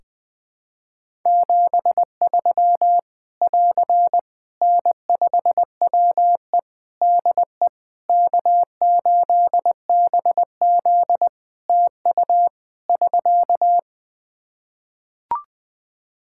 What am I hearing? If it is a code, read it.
Morse code, heard as a keyed tone, about 20 wpm, 700 Hz. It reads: 73 <AR> N5WE DE K8BZ TU <SK>
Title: K1USN SST QSO Practice 20wpm